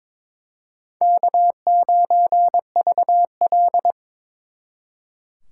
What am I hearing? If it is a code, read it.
Morse code, heard as a keyed tone, about 26 wpm, 700 Hz. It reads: K9VL
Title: Speed Racing - USA 1x2 Call Signs 22wpm